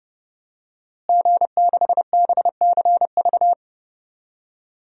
Morse code, heard as a keyed tone, about 30 wpm, 700 Hz. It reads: G6BCV
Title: DXCC-100 Call Signs - 5 Characters - Repeated 3x - 30wpm